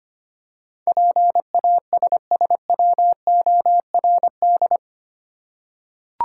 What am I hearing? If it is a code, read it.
Morse code, heard as a keyed tone, about 25 wpm, 700 Hz. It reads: PASSWORD
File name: 8 Letter Words - First 3 Letters Given - 25wpm